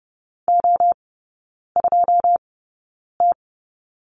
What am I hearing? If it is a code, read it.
Morse code, heard as a keyed tone, about 30 wpm, spaced as for 9 wpm, 700 Hz. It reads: O2T
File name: 081 Lesson 14e - Number 2 - Three Characters 30wpm